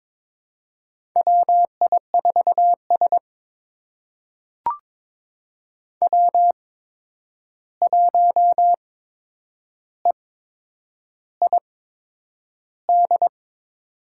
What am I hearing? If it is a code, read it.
Morse code, heard as a keyed tone, about 22 wpm, spaced as for 9 wpm, 700 Hz. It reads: WI4S W1EID
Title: USA Call Signs - 8x Character Spacing - 22wpm